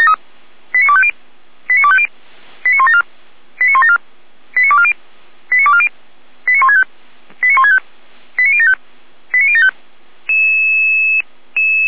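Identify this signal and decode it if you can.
Five-tone selective call (ZVEI), heard as quick runs of five tones, tones of about 70 ms. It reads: 89250 89250 89174 89174 89240 89240 89165 89165 89075 89075
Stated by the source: ZVEI five-tone call received on SDR, feuerwehr-orlamuende.org fire brigade.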